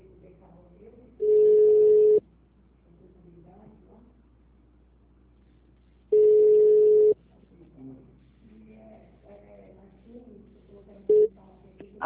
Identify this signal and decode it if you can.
Ringback tone (Europe, 425 Hz), with speech.
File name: out- -2007- - - .232